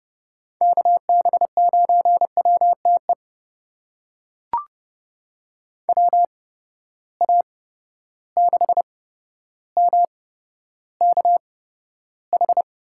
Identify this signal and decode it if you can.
Morse code, heard as a keyed tone, about 30 wpm, spaced as for 13 wpm, 700 Hz. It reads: KB9WTE WA6MKH
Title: USA Call Signs - 6 Characters - 8x Character Spacing - 30wpm